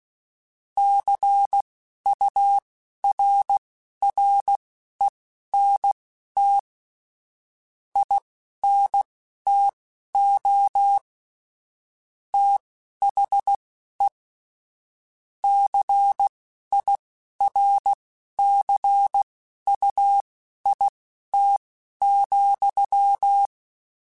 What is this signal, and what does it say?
Morse code, heard as a keyed tone, about 16 wpm, spaced as for 11 wpm, 800 Hz. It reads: CURRENT INTO THE CIRCUIT,
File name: Technical Subjects / Technical Subjects-mp3s / XC @10